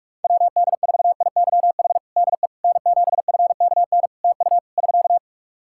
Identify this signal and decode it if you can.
Morse code, heard as a keyed tone, about 46 wpm, 700 Hz. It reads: WD4IYH DE N7FKN TU <SK>
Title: Practice QSO - Exchange 4 - Ending - 45wpm